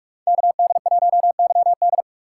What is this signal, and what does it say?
Morse code, heard as a keyed tone, about 44 wpm, 700 Hz. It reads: KD1YD
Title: USA Call Signs - 5 Characters - 3x Character Spacing - 45wpm